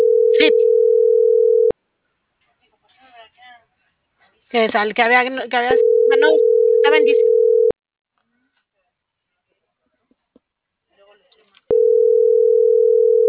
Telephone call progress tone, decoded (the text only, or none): ringback tone